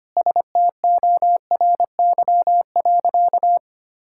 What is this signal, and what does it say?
Morse code, heard as a keyed tone, about 25 wpm, 700 Hz. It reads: STORY.
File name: practice / 25wpm / Sentences with 4-Word Phrases - 25wpm